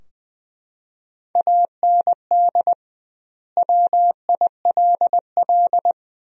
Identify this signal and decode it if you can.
Morse code, heard as a keyed tone, about 20 wpm, 700 Hz. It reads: AND WILL
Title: 2x Word-Spacing - Sentences from Top 400 Words - 20wpm